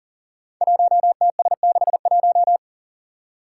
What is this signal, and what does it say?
Morse code, heard as a keyed tone, about 43 wpm, 700 Hz. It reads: 1TS61
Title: Five Challenging Letter-Number Combinations 40wpm